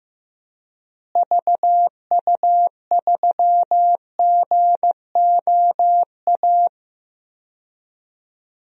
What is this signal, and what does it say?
Morse code, heard as a keyed tone, about 15 wpm, 700 Hz. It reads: VU3GOA
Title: DXCC-100 Call Signs - 6 Characters - Repeated 2x - 15wpm